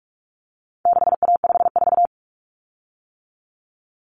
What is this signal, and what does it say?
Morse code, heard as a keyed tone, about 44 wpm, 700 Hz. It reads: 6A54
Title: Common Cut Numbers - 4-Digit 45wpm